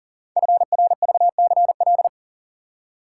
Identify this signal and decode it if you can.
Morse code, heard as a keyed tone, about 40 wpm, 700 Hz. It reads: FRVCL
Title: Code Groups - 5 Letters - 40wpm